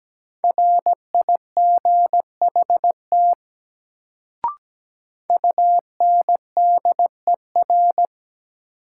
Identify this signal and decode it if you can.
Morse code, heard as a keyed tone, about 17 wpm, 700 Hz. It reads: RIGHT UNDER
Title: Sets of 2 Words - New Word Drill - 17wpm